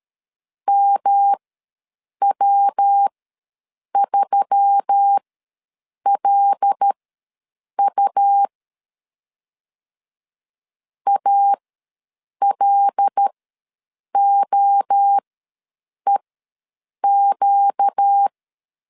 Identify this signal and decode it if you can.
Morse code, heard as a keyed tone, about 13 wpm, spaced as for 7 wpm, 800 Hz. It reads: MW3LU ALOEQ
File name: LEZIONI / 5 LBVC / 5 L ws 12wpm 2958